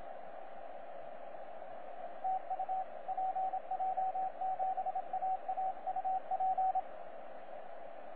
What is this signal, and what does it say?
Morse code, heard as a keyed tone, about 27 wpm, 760 Hz, with radio noise. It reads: TU PJ7AA UP
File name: PJ7AA 17CW